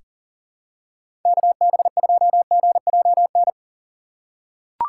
Morse code, heard as a keyed tone, about 40 wpm, 700 Hz. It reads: KB2GJN
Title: USA Call Signs - 3x Character Spacing - 40wpm